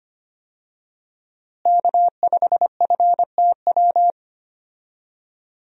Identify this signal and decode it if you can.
Morse code, heard as a keyed tone, about 25 wpm, 700 Hz. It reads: K5FTW
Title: USA Call Signs - 5 Characters - Repeated 3x - 25wpm